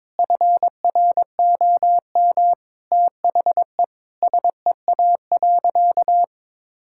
Morse code, heard as a keyed tone, about 22 wpm, 700 Hz. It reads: FROM THE SEA.
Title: Sentences from Top 1000 Words - Review Part 6 - 22wpm